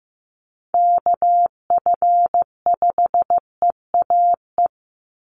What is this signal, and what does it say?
Morse code, heard as a keyed tone, about 15 wpm, 700 Hz. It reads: KF5EAE
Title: USA Call Signs 15wpm